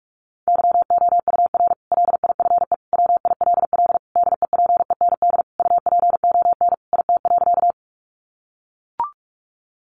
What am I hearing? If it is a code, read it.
Morse code, heard as a keyed tone, about 44 wpm, 700 Hz. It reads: YOUR LIFE WILL DEPEND UPON IT.
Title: Sentences from Top 1300 Words 45wpm